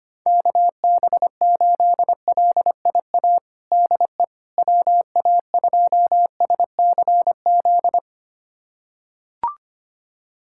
Morse code, heard as a keyed tone, about 25 wpm, 700 Hz. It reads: KB8LIA DE WA2SCZ
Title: USA Callsign DE Callsign - 6 Characters - 25wpm